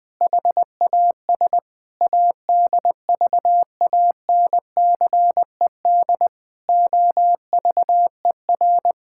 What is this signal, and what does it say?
Morse code, heard as a keyed tone, about 20 wpm, 700 Hz. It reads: HAS ADVANCED OVER
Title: Sentences with 6-Word Phrases - 20wpm